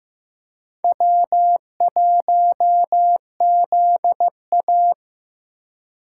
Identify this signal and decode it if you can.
Morse code, heard as a keyed tone, about 15 wpm, 700 Hz. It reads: W1ZA
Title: USA Call Signs - 3x Character Spacing - 15wpm